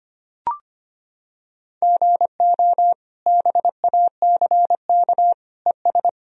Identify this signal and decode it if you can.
Morse code, heard as a keyed tone, about 25 wpm, 700 Hz. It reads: GO BACK ES
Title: Sentences from Top 300 Words and Abbreviations 25wpm